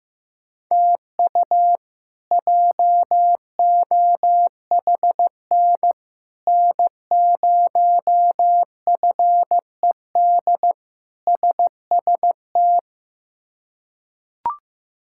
Morse code, heard as a keyed tone, about 15 wpm, 700 Hz. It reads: TU JOHN N0FED SST
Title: K1USN SST QSO Practice - 15wpm